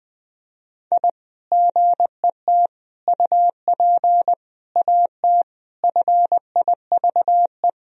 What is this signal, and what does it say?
Morse code, heard as a keyed tone, about 20 wpm, 700 Hz. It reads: I GET UP AT FIVE
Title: Sentences from Top 1000 Words - Review Part 2 - 20wpm